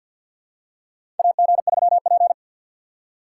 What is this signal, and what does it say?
Morse code, heard as a keyed tone, about 53 wpm, 700 Hz. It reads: AG3P
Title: USA Call Signs - 4 Characters - 6x Character Spacing - 50wpm